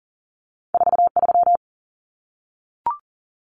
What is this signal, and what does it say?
Morse code, heard as a keyed tone, about 40 wpm, 700 Hz. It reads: 43
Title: Common Cut Numbers - 2-Digit 40wpm